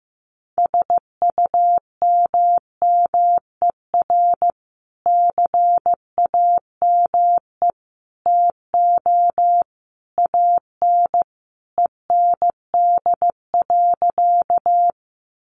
Morse code, heard as a keyed tone, about 15 wpm, 700 Hz. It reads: SUMMER CAME TO AN END.
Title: Sentences from Top 1100 Words 15wpm